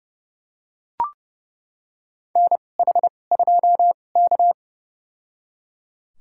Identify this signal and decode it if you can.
Morse code, heard as a keyed tone, about 30 wpm, spaced as for 22 wpm, 700 Hz. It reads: NH2K
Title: USA Call Signs - 2x Character Spacing - 30wpm